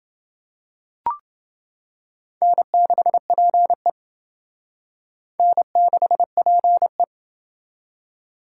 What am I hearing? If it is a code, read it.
Morse code, heard as a keyed tone, about 29 wpm, 700 Hz. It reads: N6PE N6PE